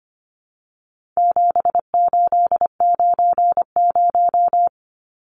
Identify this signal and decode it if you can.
Morse code, heard as a keyed tone, about 25 wpm, 700 Hz. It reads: 7890
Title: Numbers - 4-Digit 25wpm